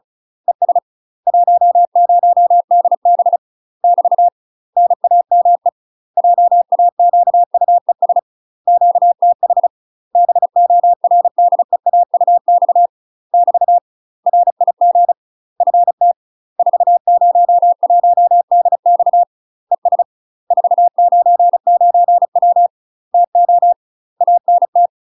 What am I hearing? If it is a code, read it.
Morse code, heard as a keyed tone, about 34 wpm, 700 Hz. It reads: ES 10DB <BT> NAME JAQUES QTH BORDEAUX <BT> RIG FT 401DX ES 499W TO ANT
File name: UK MASTER QSOs TRAINING / 35WPM / UK MASTER QSOs 0001